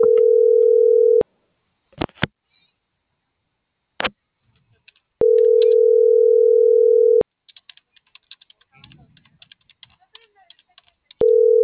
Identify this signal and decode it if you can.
Ringback tone (North America), 440 + 480 Hz.